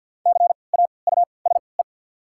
Morse code, heard as a keyed tone, about 52 wpm, spaced as for 29 wpm, 700 Hz. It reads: CAUSE
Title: 3x Character Spacing - Top 200 Words - 50wpm